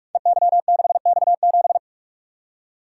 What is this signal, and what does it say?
Morse code, heard as a keyed tone, about 44 wpm, 700 Hz. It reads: EY6X7
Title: Code Groups - 5 Letter:Numbers - 45wpm